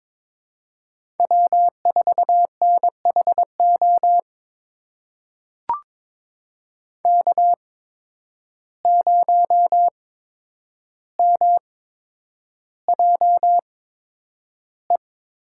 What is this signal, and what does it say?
Morse code, heard as a keyed tone, about 22 wpm, spaced as for 9 wpm, 700 Hz. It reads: W4NHO K0MJE
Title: USA Call Signs - 5 Characters - 8x Character Spacing - 22wpm